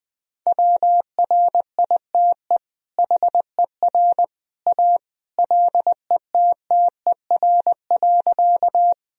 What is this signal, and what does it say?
Morse code, heard as a keyed tone, about 20 wpm, 700 Hz. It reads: WRITE HER A LETTER.
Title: Sentences from Top 1000 Words - Review Part 1 - 20wpm